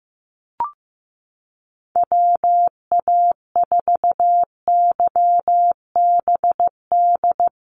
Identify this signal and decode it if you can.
Morse code, heard as a keyed tone, about 15 wpm, 700 Hz. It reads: WA4YBD
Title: USA Call Signs Repeated 3x 15wpm